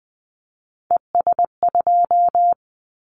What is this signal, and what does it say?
Morse code, heard as a keyed tone, about 20 wpm, 700 Hz. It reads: ES2
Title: Callsign Prefixes for Top 50 DXCC - 20wpm